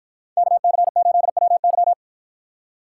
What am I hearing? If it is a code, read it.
Morse code, heard as a keyed tone, about 53 wpm, 700 Hz. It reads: KC8WY